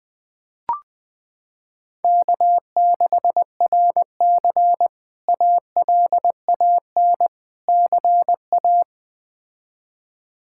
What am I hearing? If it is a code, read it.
Morse code, heard as a keyed tone, about 20 wpm, 700 Hz. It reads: K6RC ALAN CA
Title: K1USN SST QSO Practice - 20wpm